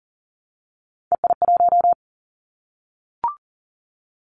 Morse code, heard as a keyed tone, about 39 wpm, 700 Hz. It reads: EI1
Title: Callsign Prefixes for Top 50 DXCC - 40wpm